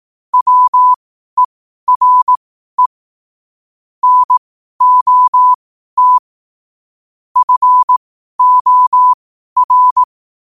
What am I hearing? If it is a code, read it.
Morse code, heard as a keyed tone, about 18 wpm, spaced as for 12 wpm, 1000 Hz. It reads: WERE NOT FOR
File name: qotdfiles / 12wpm